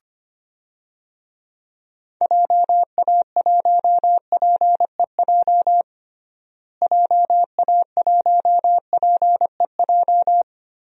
Morse code, heard as a keyed tone, about 25 wpm, 700 Hz. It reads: JA1PEJ JA1PEJ
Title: DXCC-100 Call Signs - 6 Characters - Repeated 3x - 25wpm